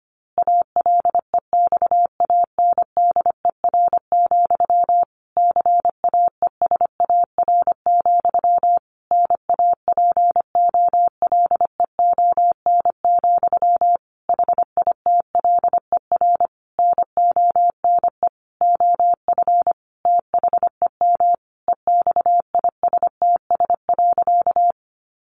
Morse code, heard as a keyed tone, about 25 wpm, 700 Hz. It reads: ALEXANDER, CAESAR, NAPOLEON, HITLER NONE OF THEM EXISTS.
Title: A Sound Of Thunder 25wpm